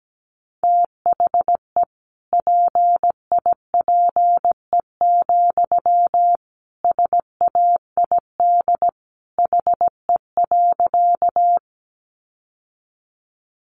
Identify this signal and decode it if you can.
Morse code, heard as a keyed tone, about 17 wpm, 700 Hz. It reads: THE PIPE, SAID HE.
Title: Sentences from Top 1500 Words 17wpm